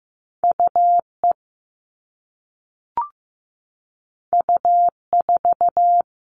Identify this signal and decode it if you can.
Morse code, heard as a keyed tone, about 15 wpm, 700 Hz. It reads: UE U4
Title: All Cut Numbers - 2-Digit 15wpm